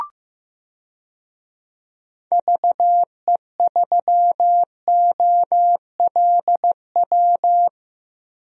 Morse code, heard as a keyed tone, about 16 wpm, 700 Hz. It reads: VE3OLW
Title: DXCC-100 Call Signs - 6 Characters - Repeated 2x - 15wpm